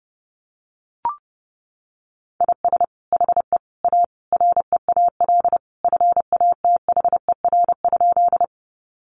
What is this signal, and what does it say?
Morse code, heard as a keyed tone, about 30 wpm, 700 Hz. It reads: IS HE A REAL FATHER?
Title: Sentences from Top 300 Words 30wpm